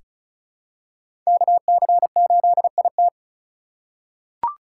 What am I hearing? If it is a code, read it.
Morse code, heard as a keyed tone, about 34 wpm, 700 Hz. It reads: KC8IT
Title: USA Call Signs - 4x Character Spacing - 35wpm